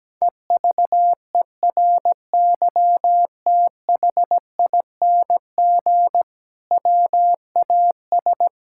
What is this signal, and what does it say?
Morse code, heard as a keyed tone, about 17 wpm, 700 Hz. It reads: EVERYTHING WAS
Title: Sentences from Top 1600 Words 17wpm